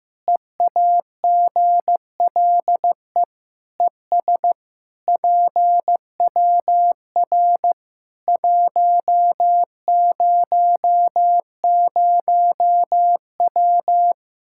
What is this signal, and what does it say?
Morse code, heard as a keyed tone, about 15 wpm, 700 Hz. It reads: EAGLE ES PWR 100W
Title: Practice QSO - Exchange 2 - Info - 15wpm